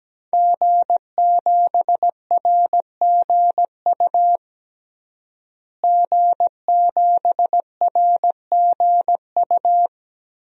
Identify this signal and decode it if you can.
Morse code, heard as a keyed tone, about 17 wpm, 700 Hz. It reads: G7RGU G7RGU
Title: DXCC-100 Call Signs - 5 Characters - Repeated 3x - 17wpm